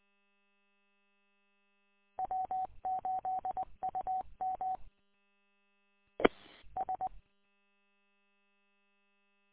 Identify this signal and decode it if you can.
Morse code, heard as a keyed tone, about 20 wpm, 750 Hz. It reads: W8UM E S